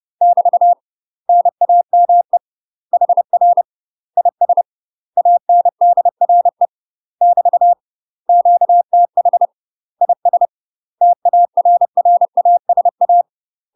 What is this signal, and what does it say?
Morse code, heard as a keyed tone, about 30 wpm, 700 Hz. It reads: <BT> NAME HR IS ANDRE <BT> QTH IS TARRASA
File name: UK MASTER QSOs TRAINING/30WPM